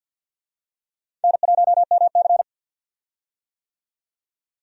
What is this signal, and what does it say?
Morse code, heard as a keyed tone, about 52 wpm, 700 Hz. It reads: N1MC
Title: USA Call Signs - 4 Characters - 50wpm